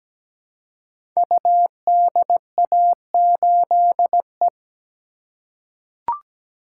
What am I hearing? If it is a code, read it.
Morse code, heard as a keyed tone, about 17 wpm, 700 Hz. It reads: UDA8E
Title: Code Groups - 5 Letter:Numbers - 17wpm